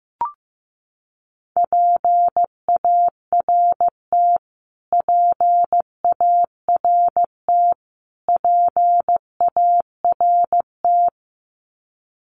Morse code, heard as a keyed tone, about 15 wpm, 700 Hz. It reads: PART PART PART
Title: Top 200 Words Repeated 15wpm